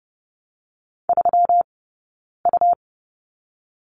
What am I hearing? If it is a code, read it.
Morse code, heard as a keyed tone, about 31 wpm, spaced as for 9 wpm, 700 Hz. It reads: 3U
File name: Lesson 20d - Number 3 - Two Characters 30wpm